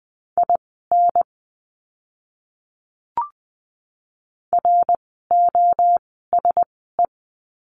Morse code, heard as a keyed tone, about 20 wpm, spaced as for 14 wpm, 700 Hz. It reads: IN ROSE
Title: 061 Lesson 11f - Letter h - Words 20wpm